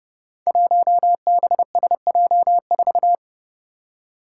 Morse code, heard as a keyed tone, about 30 wpm, 700 Hz. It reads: 1BSJ4
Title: Five Challenging Letter-Number Combinations 30wpm